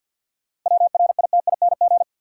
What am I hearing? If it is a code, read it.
Morse code, heard as a keyed tone, about 53 wpm, 700 Hz. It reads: WRITING
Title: Seven Letter Words - 50wpm